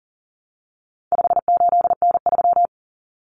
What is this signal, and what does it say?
Morse code, heard as a keyed tone, about 40 wpm, 700 Hz. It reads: 58N3
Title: Common Cut Numbers - 4-Digit 40wpm